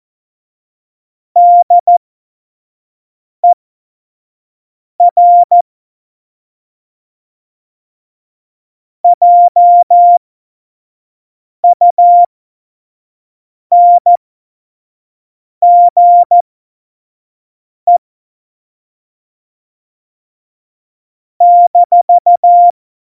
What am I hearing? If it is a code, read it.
Morse code, heard as a keyed tone, about 14 wpm, spaced as for 5 wpm, 700 Hz. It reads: DER JUNGE -
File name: Morsekurs / Witze / Witz 15. Dezember, F70